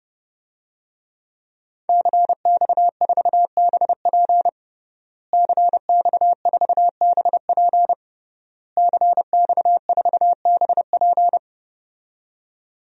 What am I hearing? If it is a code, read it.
Morse code, heard as a keyed tone, about 30 wpm, 700 Hz. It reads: CX4BP CX4BP CX4BP
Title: DXCC-100 Call Signs - 5 Characters - Repeated 3x - 30wpm